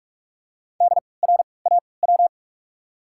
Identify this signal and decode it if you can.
Morse code, heard as a keyed tone, about 44 wpm, spaced as for 25 wpm, 700 Hz. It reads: DRAW
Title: 3x Character Spacing - Top 500 Words - 45wpm